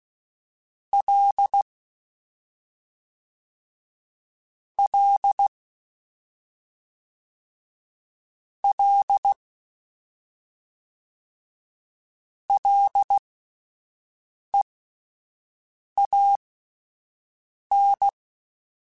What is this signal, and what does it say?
Morse code, heard as a keyed tone, about 16 wpm, spaced as for 5 wpm, 800 Hz. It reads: L L L LEAN